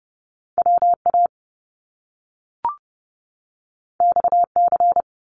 Morse code, heard as a keyed tone, about 30 wpm, 700 Hz. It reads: WA XC
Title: Top Two Letter Combinations in Words 30wpm